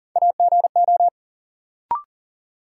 Morse code, heard as a keyed tone, about 40 wpm, 700 Hz. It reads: AGO
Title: Sets of 2 Words - New Word Drill - 40wpm